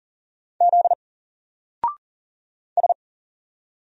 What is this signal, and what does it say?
Morse code, heard as a keyed tone, about 39 wpm, 700 Hz. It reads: Z S
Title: Single Character (Letter:Number:Punctuation:Prosign) Alt 40wpm